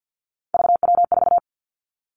Morse code, heard as a keyed tone, about 50 wpm, 700 Hz. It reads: VW4
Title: Three Challenging Letter-Number Combinations 50wpm